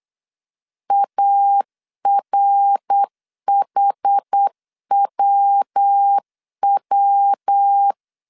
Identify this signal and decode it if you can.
Morse code, heard as a keyed tone, about 8 wpm, 800 Hz. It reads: ARHWW